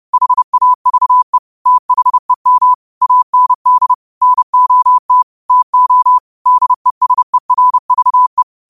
Morse code, heard as a keyed tone, about 30 wpm, 1000 Hz. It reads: HAVE THEM AND NOT TO DESERVE
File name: 30wpm